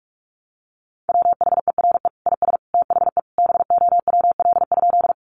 Morse code, heard as a keyed tone, about 44 wpm, 700 Hz. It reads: WHERE IS THE BOWL?
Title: Sentences from Top 1200 Words 45wpm